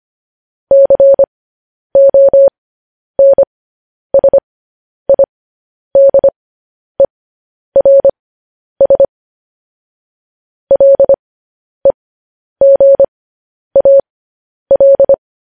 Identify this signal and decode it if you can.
Morse code, heard as a keyed tone, about 25 wpm, spaced as for 10 wpm, 550 Hz. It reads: CONSIDERS LEGAL
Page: news-headlines-in-morse-code-at-10-wpm